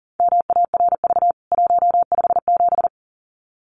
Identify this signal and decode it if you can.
Morse code, heard as a keyed tone, about 37 wpm, 700 Hz. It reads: MARV 157
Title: CWT Exchange 40wpm